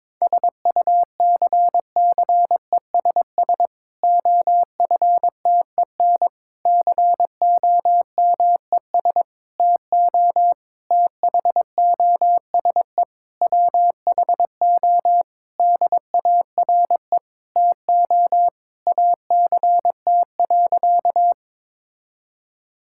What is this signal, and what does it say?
Morse code, heard as a keyed tone, about 21 wpm, 700 Hz. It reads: SUCCESS OFTEN COMES TO THOSE WHO DARE TO ACT.
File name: Difficult Sentences with Hints - 22wpm